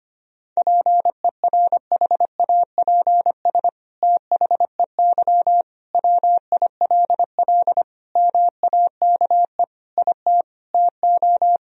Morse code, heard as a keyed tone, about 25 wpm, 700 Hz. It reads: PERHAPS THEY WILL MAKE IT TO
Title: Sentences from Top 1000 Words - Review Part 2 - 25wpm